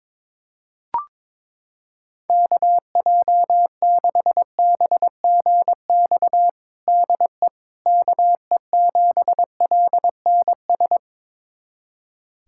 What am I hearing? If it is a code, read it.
Morse code, heard as a keyed tone, about 22 wpm, 700 Hz. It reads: KJ6BGX DE KE7LNS
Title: USA Callsign DE Callsign - 6 Characters - 22wpm